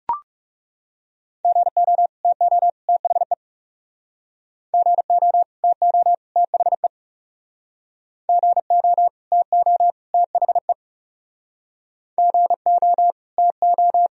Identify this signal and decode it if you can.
Morse code, heard as a keyed tone, about 40 wpm, 700 Hz. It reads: GO TO THE GO TO THE GO TO THE GO TO